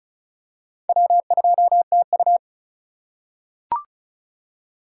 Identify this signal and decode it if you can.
Morse code, heard as a keyed tone, about 34 wpm, 700 Hz. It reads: W2TU
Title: USA Call Signs - 4 Characters - 2x Character Spacing - 35wpm